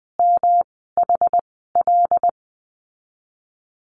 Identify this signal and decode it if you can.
Morse code, heard as a keyed tone, about 20 wpm, spaced as for 15 wpm, 700 Hz. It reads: MHL
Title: 207 Lesson 33g - Letter j - Call Signs 20wpm